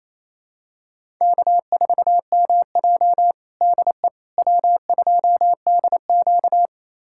Morse code, heard as a keyed tone, about 28 wpm, 700 Hz. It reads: K4MJ DE W2DQ
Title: USA Callsign DE Callsign - 4 Characters - 28wpm